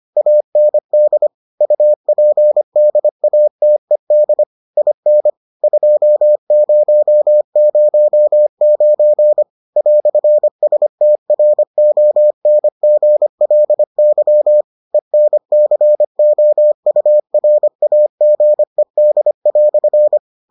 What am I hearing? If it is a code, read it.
Morse code, heard as a keyed tone, about 25 wpm, 600 Hz. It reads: AND UPDATED IN 2009 "STRONGLY ENCOURAGED"